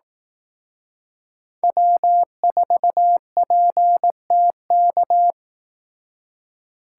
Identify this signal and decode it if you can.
Morse code, heard as a keyed tone, about 18 wpm, 700 Hz. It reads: W4PTK
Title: Speed Racing - USA 1x3 Call Signs 15wpm